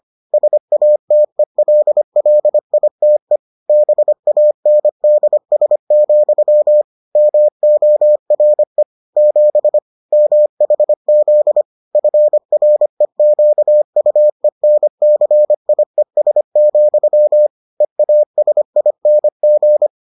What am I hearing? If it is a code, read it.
Morse code, heard as a keyed tone, about 25 wpm, 600 Hz. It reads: SATELLITE BANDS, MORE 7 MHZ FREQUENCIES, EASING